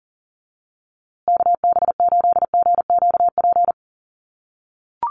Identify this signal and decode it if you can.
Morse code, heard as a keyed tone, about 40 wpm, 700 Hz. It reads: KB8GQP
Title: USA Call Signs Repeated 3x 40wpm